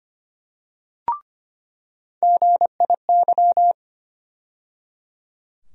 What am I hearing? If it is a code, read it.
Morse code, heard as a keyed tone, about 25 wpm, 700 Hz. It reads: GIY
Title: USA Call Sign Postfixes - 3 Letters 25wpm